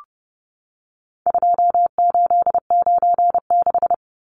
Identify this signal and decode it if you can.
Morse code, heard as a keyed tone, about 30 wpm, 700 Hz. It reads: 2896
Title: Numbers - 4-Digit 30wpm